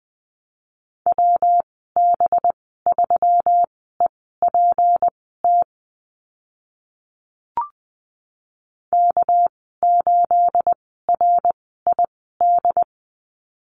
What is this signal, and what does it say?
Morse code, heard as a keyed tone, about 20 wpm, spaced as for 14 wpm, 700 Hz. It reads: WB3EPT K8RID
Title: 219 Lesson 35g - Number 0 - Call Signs 20wpm